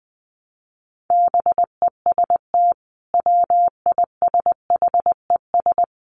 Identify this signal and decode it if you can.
Morse code, heard as a keyed tone, about 20 wpm, 700 Hz. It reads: BEST WISHES
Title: Common QSO Phrases 20wpm